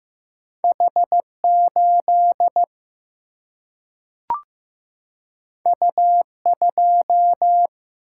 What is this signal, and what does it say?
Morse code, heard as a keyed tone, about 15 wpm, 700 Hz. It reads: H8 U2
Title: Single-Letter Single-Number 15wpm